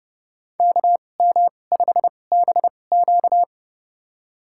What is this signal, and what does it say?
Morse code, heard as a keyed tone, about 29 wpm, spaced as for 22 wpm, 700 Hz. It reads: KM5BQ
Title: USA Call Signs - 5 Characters - 2x Character Spacing - 30wpm